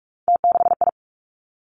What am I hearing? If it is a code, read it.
Morse code, heard as a keyed tone, about 46 wpm, 700 Hz. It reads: T6I
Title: Three Challenging Letter-Number Combinations 45wpm